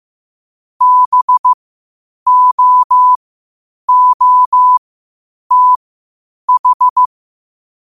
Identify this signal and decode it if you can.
Morse code, heard as a keyed tone, about 15 wpm, spaced as for 8 wpm, 1000 Hz. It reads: BOOTH
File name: qotdfiles / 8wpm